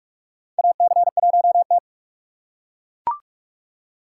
Morse code, heard as a keyed tone, about 42 wpm, 700 Hz. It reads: AC1T
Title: USA Call Signs - 4 Characters - 6x Character Spacing - 45wpm